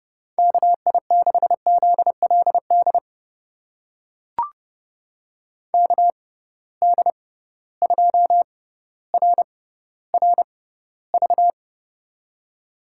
Morse code, heard as a keyed tone, about 30 wpm, spaced as for 15 wpm, 700 Hz. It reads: KI6ZLD KD2RRV
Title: USA Call Signs - 6x Character Spacing - 30wpm